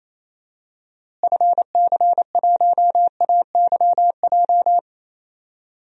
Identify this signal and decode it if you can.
Morse code, heard as a keyed tone, about 27 wpm, 700 Hz. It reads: FC1AYJ
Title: DXCC-100 Call Signs - 6 Characters - Repeated 3x - 28wpm